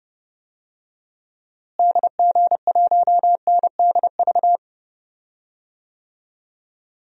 Morse code, heard as a keyed tone, about 28 wpm, 700 Hz. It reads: DG1NDV
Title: DXCC-100 Call Signs - 6 Characters - Repeated 3x - 30wpm